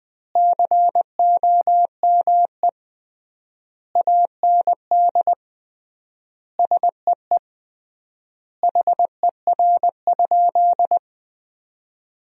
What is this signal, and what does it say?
Morse code, heard as a keyed tone, about 21 wpm, 700 Hz. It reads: COME AND SEE HER?
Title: 3x Word-Spacing - Sentences from Top 100 Words - 20wpm